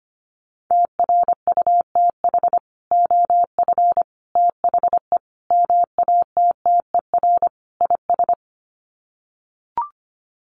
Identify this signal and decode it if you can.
Morse code, heard as a keyed tone, about 25 wpm, 700 Hz. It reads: TRUTH OF THE MATTER IS
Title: Sets of 5 Words Encore2 25wpm